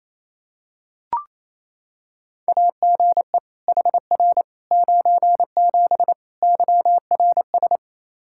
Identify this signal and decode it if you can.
Morse code, heard as a keyed tone, about 28 wpm, 700 Hz. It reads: AGE HR 97 YRS
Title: Practice QSO - Exchange 3 - More Info - 28wpm